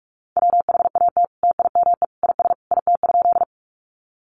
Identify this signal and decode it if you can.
Morse code, heard as a keyed tone, about 44 wpm, 700 Hz. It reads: WHAT TIME IS IT?
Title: Sentences from Top 100 Words 45wpm